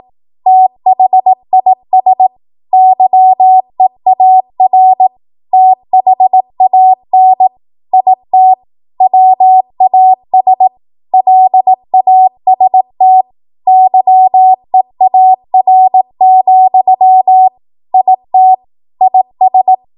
Morse code, heard as a keyed tone, about 18 wpm, 750 Hz. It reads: THIS YEAR THAN IT WAS LAST YEAR, IT IS
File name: Morse / 18WPM